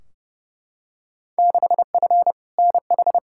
Morse code, heard as a keyed tone, about 31 wpm, 700 Hz. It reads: 6F NH
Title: Field Day Exchanges Encore - 30wpm